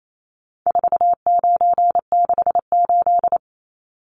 Morse code, heard as a keyed tone, about 28 wpm, 700 Hz. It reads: 4968
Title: Numbers - 4-Digit 28wpm